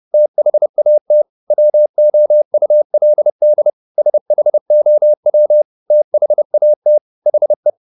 Morse code, heard as a keyed tone, about 30 wpm, 600 Hz. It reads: THAT WOULD SHOW THAT HE
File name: the old man and the sea / the old man and the sea 1 30wpm